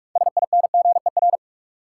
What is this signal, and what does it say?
Morse code, heard as a keyed tone, about 46 wpm, 700 Hz. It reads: SINGER
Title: Six Letter Words - 45wpm